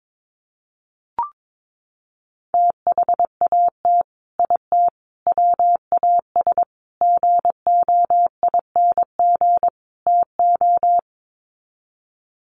Morse code, heard as a keyed tone, about 22 wpm, 700 Hz. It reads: THAT IT WAS GOING TO
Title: Sets of 5 Words Encore2 22wpm